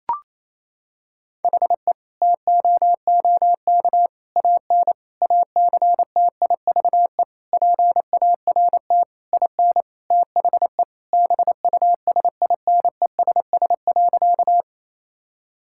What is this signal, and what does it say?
Morse code, heard as a keyed tone, about 28 wpm, 700 Hz. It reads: HE TOOK AN ACTIVE PART IN THE BUSINESS.
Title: Sentences from Top 1000 Words - Review Part 6 - 28wpm